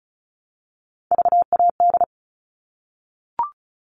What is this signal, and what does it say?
Morse code, heard as a keyed tone, about 34 wpm, 700 Hz. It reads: VAD
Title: All Cut Numbers - 3-Digit 35wpm